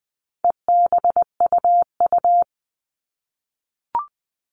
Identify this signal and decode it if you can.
Morse code, heard as a keyed tone, about 20 wpm, 700 Hz. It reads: EBUU
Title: All Cut Numbers - 4-Digit 20wpm